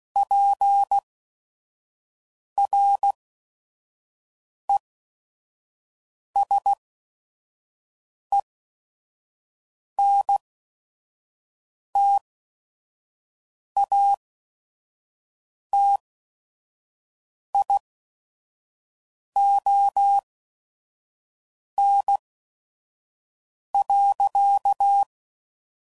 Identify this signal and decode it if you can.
Morse code, heard as a keyed tone, about 16 wpm, spaced as for 5 wpm, 800 Hz. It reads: PRESENTATION.
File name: Sayings / Quotes / Persperation